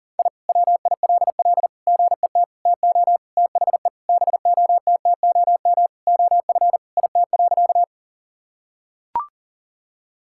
Morse code, heard as a keyed tone, about 37 wpm, 700 Hz. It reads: I WILL GET TO THE BOTTOM OF IT.
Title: Sentences from Top 1000 Words - Review Part 6 - 40wpm